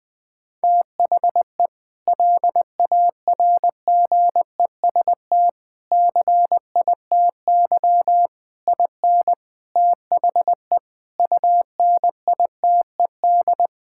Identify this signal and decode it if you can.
Morse code, heard as a keyed tone, about 20 wpm, 700 Hz. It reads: THE LARGEST CITY IN THE UNITED
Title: Sentences with 5-Word Phrases - Spoken Beginning - 20wpm